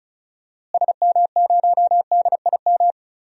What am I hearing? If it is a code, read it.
Morse code, heard as a keyed tone, about 36 wpm, 700 Hz. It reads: SM0DIM
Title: DXCC-100 Call Signs - 6 Characters - Repeated 2x - 35wpm